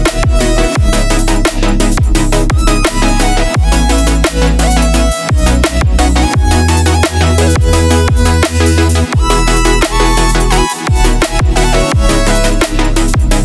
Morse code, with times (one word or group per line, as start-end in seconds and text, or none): none